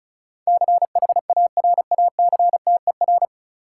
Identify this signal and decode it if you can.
Morse code, heard as a keyed tone, about 36 wpm, 700 Hz. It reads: CHARACTER
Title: Sentences from Top 1200 Words - New Word Drill Alt - 35wpm